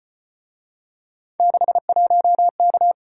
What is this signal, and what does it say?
Morse code, heard as a keyed tone, about 33 wpm, 700 Hz. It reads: 61K